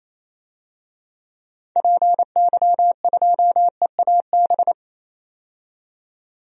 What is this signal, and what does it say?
Morse code, heard as a keyed tone, about 28 wpm, 700 Hz. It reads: PY2EAB
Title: DXCC-100 Call Signs - 6 Characters - 28wpm